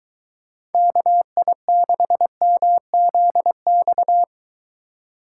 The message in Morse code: KI6MZX

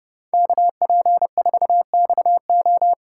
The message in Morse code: KP4XO

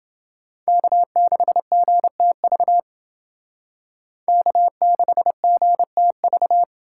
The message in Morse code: K6GTV K6GTV

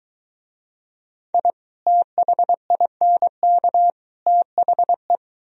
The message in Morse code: I THINK THE

